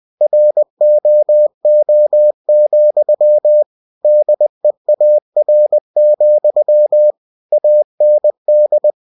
Morse code, heard as a keyed tone, about 20 wpm, 600 Hz. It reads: ROO, DEAR, AND
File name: cw-books / winnie-the-pooh / winnie-20wpm-page-0069